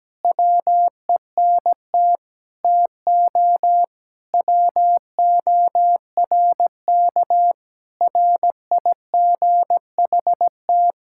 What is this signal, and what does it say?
Morse code, heard as a keyed tone, about 17 wpm, 700 Hz. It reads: WENT TO WORK RIGHT
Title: Sentences from Top 1600 Words 17wpm